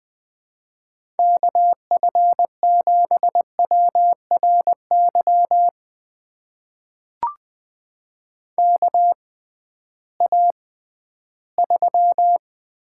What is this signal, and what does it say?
Morse code, heard as a keyed tone, about 20 wpm, spaced as for 13 wpm, 700 Hz. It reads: KF7WRY KA3
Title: USA Call Signs - 6 Characters - 6x Character Spacing - 20wpm